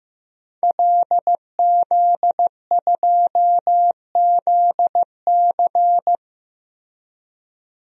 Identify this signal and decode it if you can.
Morse code, heard as a keyed tone, about 15 wpm, 700 Hz. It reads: LZ2ZC
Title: DXCC-100 Call Signs - 5 Characters - Repeated 2x - 15wpm